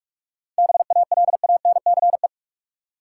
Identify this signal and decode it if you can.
Morse code, heard as a keyed tone, about 47 wpm, 700 Hz. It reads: BALANCE
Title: Sentences from Top 900 Words - New Word Drill - 45wpm